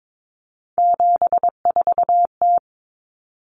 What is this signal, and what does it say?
Morse code, heard as a keyed tone, about 22 wpm, 700 Hz. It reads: 74T
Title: Common Cut Numbers - 3-Digit 22wpm